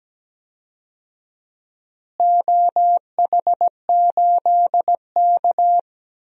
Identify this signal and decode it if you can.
Morse code, heard as a keyed tone, about 17 wpm, 700 Hz. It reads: OH8K
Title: DXCC-100 Call Signs - 4 Characters - Repeated 3x - 17wpm